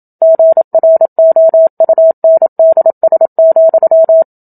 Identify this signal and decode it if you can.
Morse code, heard as a keyed tone, about 27 wpm, 650 Hz. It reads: GROUNDS,
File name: Book Two / 01-Chapter 1 - 27 WPM